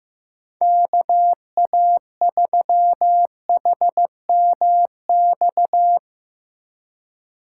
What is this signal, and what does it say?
Morse code, heard as a keyed tone, about 16 wpm, 700 Hz. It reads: KA3HMX
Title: USA Call Signs - 6 Characters - 2x Character Spacing - 15wpm